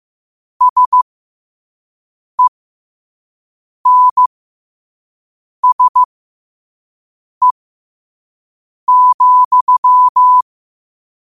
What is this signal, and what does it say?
Morse code, heard as a keyed tone, about 15 wpm, spaced as for 5 wpm, 1000 Hz. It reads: SENSE,
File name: qotdfiles / 5wpm